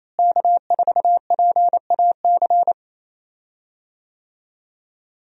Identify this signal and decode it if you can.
Morse code, heard as a keyed tone, about 28 wpm, 700 Hz. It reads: K4PAC
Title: USA Call Signs - 5 Characters - Repeated 3x - 28wpm